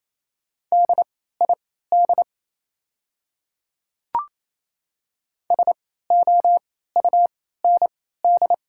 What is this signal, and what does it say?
Morse code, heard as a keyed tone, about 28 wpm, spaced as for 16 wpm, 700 Hz. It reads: DID SOUND
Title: 3x Character Spacing - Top 100 Words - 28wpm